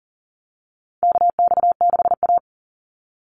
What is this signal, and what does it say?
Morse code, heard as a keyed tone, about 40 wpm, 700 Hz. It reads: KX6A
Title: NA SOTA Chaser Callsigns 40wpm